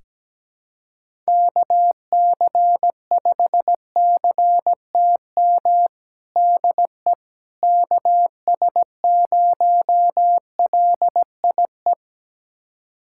Morse code, heard as a keyed tone, about 17 wpm, 700 Hz. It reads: KC5CTM DE KS0LIE
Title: USA Callsign DE Callsign - 6 Characters - 17wpm